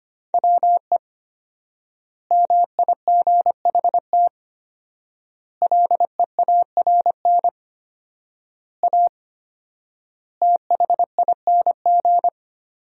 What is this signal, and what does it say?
Morse code, heard as a keyed tone, about 25 wpm, 700 Hz. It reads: WE MIGHT LEARN A THING
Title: 4x Word-Spacing - Sentences from Top 300 Words - 25wpm